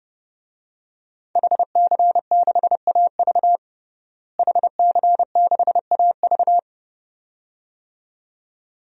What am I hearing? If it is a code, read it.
Morse code, heard as a keyed tone, about 30 wpm, 700 Hz. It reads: HC6AV HC6AV
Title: DXCC-100 Call Signs - 5 Characters - Repeated 2x - 30wpm